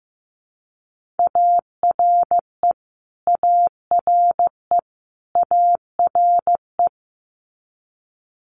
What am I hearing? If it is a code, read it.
Morse code, heard as a keyed tone, about 15 wpm, 700 Hz. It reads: ARE ARE ARE
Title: Top 200 Words Repeated 15wpm